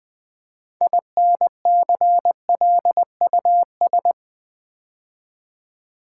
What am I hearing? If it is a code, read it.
Morse code, heard as a keyed tone, about 20 wpm, 700 Hz. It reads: INCLUS
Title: Build A Word - 3 Letters At A Time - 9-Letter Words - 20wpm